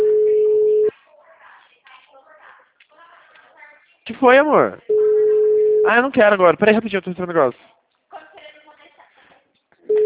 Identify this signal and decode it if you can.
Ringback tone (Europe, 425 Hz), with speech.
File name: out- -2002- - - .34333